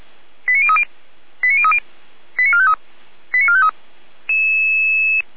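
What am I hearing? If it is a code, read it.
Five-tone selective call (ZVEI), heard as quick runs of five tones, tones of about 70 ms. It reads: 89030 89030 89453 89453
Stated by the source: ZVEI five-tone selective call 89081 received by SDR, from the Feuerwehr Orlamünde server.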